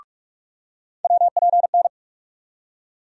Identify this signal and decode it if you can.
Morse code, heard as a keyed tone, about 43 wpm, 700 Hz. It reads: WPN
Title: USA Call Sign Postfixes - 3 Letters 45wpm